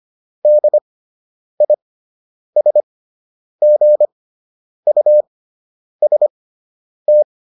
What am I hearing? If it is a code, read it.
Morse code, heard as a keyed tone, about 25 wpm, spaced as for 9 wpm, 600 Hz. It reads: DISGUST